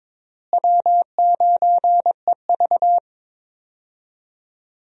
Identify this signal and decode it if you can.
Morse code, heard as a keyed tone, about 22 wpm, 700 Hz. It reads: W9EV
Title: USA Call Signs - 4 Characters - Repeated 2x - 22wpm